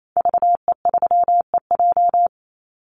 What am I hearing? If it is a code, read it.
Morse code, heard as a keyed tone, about 28 wpm, 700 Hz. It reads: VE3EJ
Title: Call Signs Repeated 2x - 28wpm